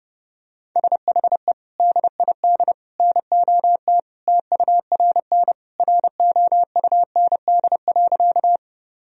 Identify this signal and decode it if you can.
Morse code, heard as a keyed tone, about 30 wpm, 700 Hz. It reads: SHE DID NOT TURN ROUND.
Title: Sentences from Top 1000 Words - Review Part 1 - 30wpm